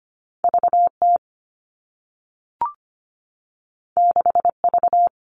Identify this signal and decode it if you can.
Morse code, heard as a keyed tone, about 25 wpm, 700 Hz. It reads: VT 6V
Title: All Cut Numbers - 2-Digit 25wpm